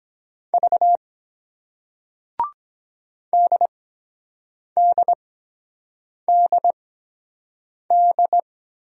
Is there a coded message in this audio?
Yes: Morse code from 0.5 to 8.4 s, heard as a keyed tone, about 23 wpm, spaced as for 7 wpm, 700 Hz. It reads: V DDDD